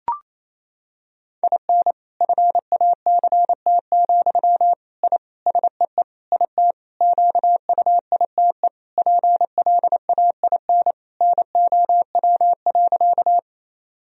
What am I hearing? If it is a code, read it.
Morse code, heard as a keyed tone, about 27 wpm, 700 Hz. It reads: IN FACT, I SEE IT QUITE PLAIN NOW.
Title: Sentences from Top 1000 Words - Review Part 3 - 28wpm